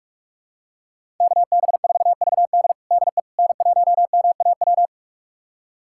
Morse code, heard as a keyed tone, about 44 wpm, 700 Hz. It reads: KB4VD DE N1MAW
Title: USA Callsign DE Callsign - 5 Characters - 45wpm